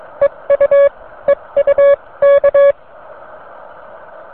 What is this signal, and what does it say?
Morse code, heard as a keyed tone, about 22 wpm, spaced as for 19 wpm, 560 Hz. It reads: EU EU K